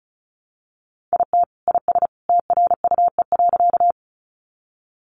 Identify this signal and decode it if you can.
Morse code, heard as a keyed tone, about 34 wpm, 700 Hz. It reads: IT IS TRUE.
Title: Sentences from Top 400 Words 35wpm